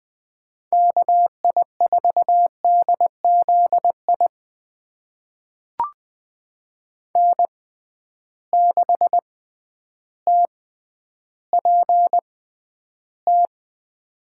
Morse code, heard as a keyed tone, about 20 wpm, spaced as for 10 wpm, 700 Hz. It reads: KI4DZ I N6TPT
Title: USA Call Signs - 6x Character Spacing - 20wpm